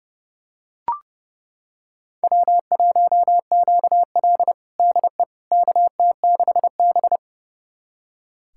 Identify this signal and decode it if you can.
Morse code, heard as a keyed tone, about 30 wpm, 700 Hz. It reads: W1QL DE KT6B
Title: USA Callsign DE Callsign - 4 Characters - 30wpm